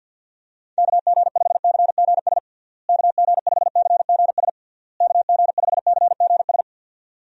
Morse code, heard as a keyed tone, about 52 wpm, 700 Hz. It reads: KG5CGS KG5CGS KG5CGS
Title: USA Call Signs - 6 Characters - Repeated 3x - 50wpm